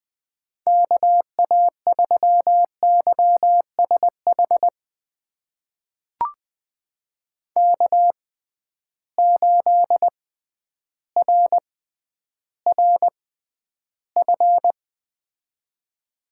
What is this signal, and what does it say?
Morse code, heard as a keyed tone, about 20 wpm, spaced as for 11 wpm, 700 Hz. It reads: KA3YSH K8RRF
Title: USA Call Signs - 6x Character Spacing - 20wpm